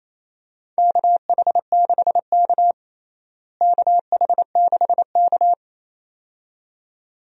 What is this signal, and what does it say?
Morse code, heard as a keyed tone, about 28 wpm, 700 Hz. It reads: KH6K KH6K
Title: DXCC-100 Call Signs - 4 Characters - Repeated 2x - 28wpm